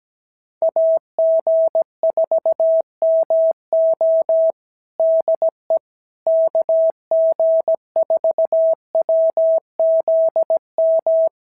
Morse code, heard as a keyed tone, about 16 wpm, 650 Hz. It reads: AG4MO DE KG4WZM
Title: SOTA QSO Practice 17wpm